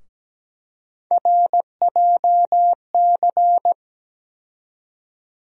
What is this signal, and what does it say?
Morse code, heard as a keyed tone, about 17 wpm, 700 Hz. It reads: RJC
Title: USA Call Sign Postfixes - 3 Letters 17wpm